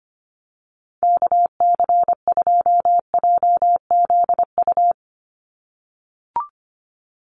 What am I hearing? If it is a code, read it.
Morse code, heard as a keyed tone, about 25 wpm, 700 Hz. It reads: KC2JZU
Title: USA Call Signs 25wpm